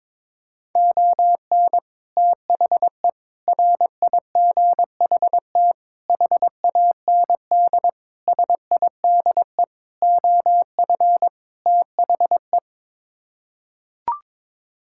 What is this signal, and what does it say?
Morse code, heard as a keyed tone, about 22 wpm, 700 Hz. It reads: ON THE RIGHT HAND SIDE OF THE
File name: practice / 22wpm / Sets of 7 Words - 22wpm